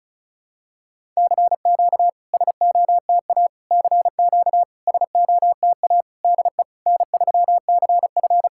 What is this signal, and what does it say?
Morse code, heard as a keyed tone, about 37 wpm, 700 Hz. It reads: CQ SOTA CQ SOTA DE N3CF
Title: SOTA QSO Practice 35wpm